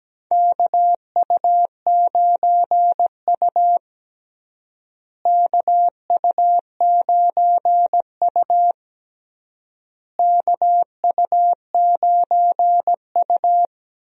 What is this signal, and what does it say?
Morse code, heard as a keyed tone, about 17 wpm, 700 Hz. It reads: KU9U KU9U KU9U
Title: USA Call Signs - 4 Characters - Repeated 3x - 17wpm